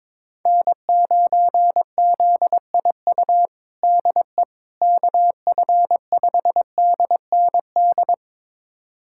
Morse code, heard as a keyed tone, about 22 wpm, 700 Hz. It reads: N9ZIU DE KF5DND
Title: USA Callsign DE Callsign 22wpm